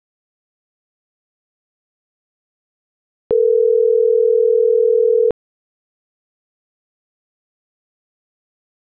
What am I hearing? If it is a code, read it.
Ringback tone (North America), 440 + 480 Hz.